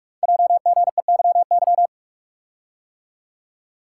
Morse code, heard as a keyed tone, about 44 wpm, 690 Hz. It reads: JGEYY
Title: Code Groups - 5 Letters - 45wpm